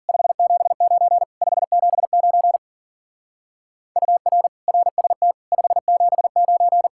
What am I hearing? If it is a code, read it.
Morse code, heard as a keyed tone, about 50 wpm, 700 Hz. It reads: 579 579 UR RST 579